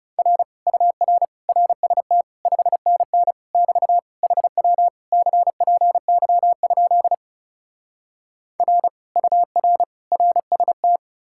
Morse code, heard as a keyed tone, about 40 wpm, 700 Hz. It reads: R UR RST 5NN <BT> HW CPY? R UR RST